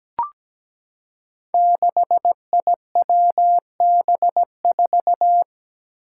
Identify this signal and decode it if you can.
Morse code, heard as a keyed tone, about 17 wpm, 700 Hz. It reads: 6IWB4
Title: Five Challenging Letter-Number Combinations 17wpm